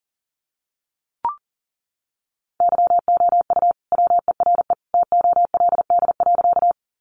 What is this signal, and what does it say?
Morse code, heard as a keyed tone, about 37 wpm, 700 Hz. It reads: YOU WERE TOLD.
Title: Sentences from Top 400 Words 40wpm